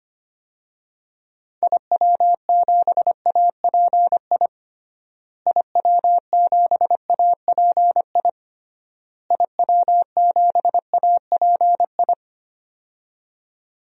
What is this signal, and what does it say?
Morse code, heard as a keyed tone, about 25 wpm, 700 Hz. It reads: IW7API IW7API IW7API
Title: DXCC-100 Call Signs - 6 Characters - Repeated 3x - 25wpm